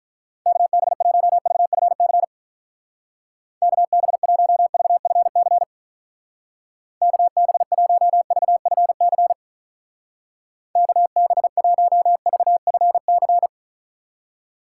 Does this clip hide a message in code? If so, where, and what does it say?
Yes: Morse code from 0.5 to 13.5 s, heard as a keyed tone, about 48 wpm, 700 Hz. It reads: KB1VFC KB1VFC KB1VFC KB1VFC